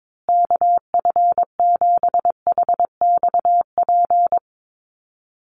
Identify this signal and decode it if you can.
Morse code, heard as a keyed tone, about 22 wpm, 700 Hz. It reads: KF7HXP
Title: USA Call Signs Repeated 3x 22wpm